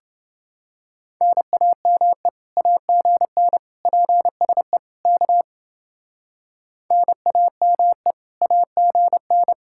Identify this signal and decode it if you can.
Morse code, heard as a keyed tone, about 29 wpm, 700 Hz. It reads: NAME AGN PSE K NAME AGN